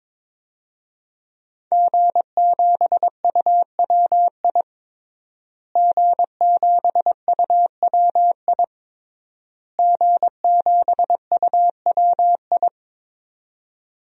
Morse code, heard as a keyed tone, about 22 wpm, 700 Hz. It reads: G7UWI G7UWI G7UWI
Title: DXCC-100 Call Signs - 5 Characters - Repeated 3x - 22wpm